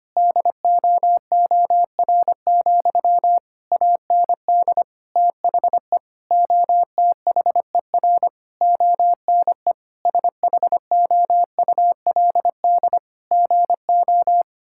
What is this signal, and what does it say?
Morse code, heard as a keyed tone, about 25 wpm, 700 Hz. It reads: DOOR, AND THE OTHER ONE SHOULD GO
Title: Sentences from Top 1600 Words 25wpm